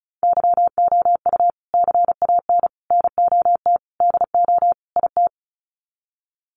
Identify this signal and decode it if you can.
Morse code, heard as a keyed tone, about 34 wpm, 700 Hz. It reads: YOU CAN NOT DO IT
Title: Sets of 5 Words Encore2 35wpm